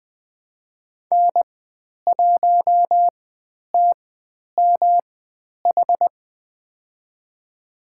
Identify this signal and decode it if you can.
Morse code, heard as a keyed tone, about 20 wpm, spaced as for 10 wpm, 700 Hz. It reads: N1TMH
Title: 109 Lesson 18g - Letter m - Call Signs 20wpm